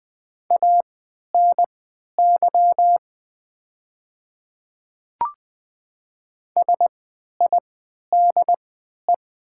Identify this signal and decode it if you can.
Morse code, heard as a keyed tone, about 20 wpm, spaced as for 11 wpm, 690 Hz. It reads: ANY SIDE
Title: 3x Character Spacing - Top 300 Words - 20wpm